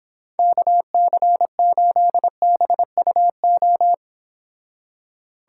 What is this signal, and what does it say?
Morse code, heard as a keyed tone, about 31 wpm, 700 Hz. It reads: KC8BUO